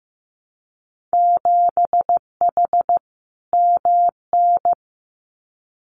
Morse code, heard as a keyed tone, about 15 wpm, 700 Hz. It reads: 7H MN
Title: Winter Field Day Exchanges Encore 15wpm